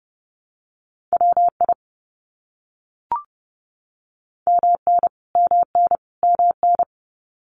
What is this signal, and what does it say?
Morse code, heard as a keyed tone, about 30 wpm, 700 Hz. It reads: WI MN MN MN
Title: US States Repeated 3x 30wpm